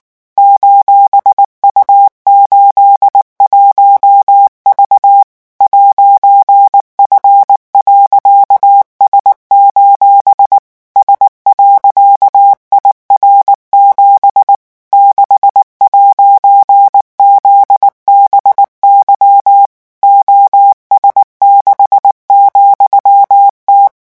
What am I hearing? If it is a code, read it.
Morse code, heard as a keyed tone, about 19 wpm, 800 Hz. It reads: :U81V 'F.S: S.IR7 6'ZBY OS6,T
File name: morse mp3 / revision14-vitesse 18 mots